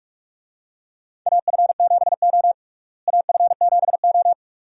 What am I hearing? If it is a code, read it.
Morse code, heard as a keyed tone, about 46 wpm, 700 Hz. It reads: AF7O AF7O
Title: USA Call Signs - 4 Characters - Repeated 2x - 45wpm